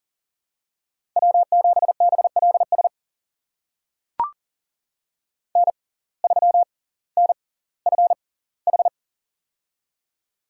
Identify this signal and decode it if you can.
Morse code, heard as a keyed tone, about 40 wpm, spaced as for 20 wpm, 700 Hz. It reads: W7BLS N3NFH
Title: USA Call Signs - 5 Characters - 6x Character Spacing - 40wpm